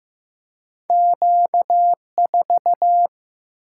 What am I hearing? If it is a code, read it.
Morse code, heard as a keyed tone, about 15 wpm, 700 Hz. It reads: Q 4
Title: Single-Letter Single-Number 15wpm